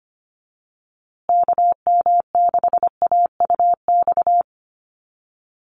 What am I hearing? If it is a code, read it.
Morse code, heard as a keyed tone, about 25 wpm, 700 Hz. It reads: KM6AUX
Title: USA Call Signs Repeated 2x 25wpm